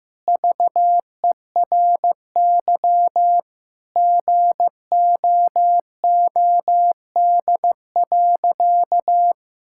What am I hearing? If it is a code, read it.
Morse code, heard as a keyed tone, about 15 wpm, 700 Hz. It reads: VERY GOOD.
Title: Sentences from Top 1000 Words - Review Part 5 - 15wpm